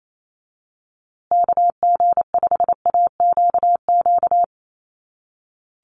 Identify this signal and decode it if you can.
Morse code, heard as a keyed tone, about 28 wpm, 700 Hz. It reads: KG5AQQ
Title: USA Call Signs Repeated 3x 28wpm